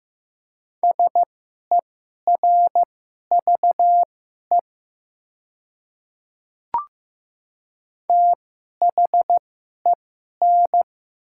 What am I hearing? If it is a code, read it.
Morse code, heard as a keyed tone, about 15 wpm, spaced as for 11 wpm, 700 Hz. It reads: SERVE THEN
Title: 2x Character Spacing - Top 500 Words - 15wpm